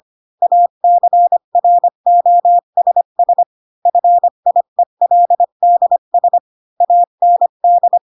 Morse code, heard as a keyed tone, about 25 wpm, 700 Hz. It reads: ACROSS FIELDS AND
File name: Pinocchio / Pinocchio ch4 20 25 ik1hge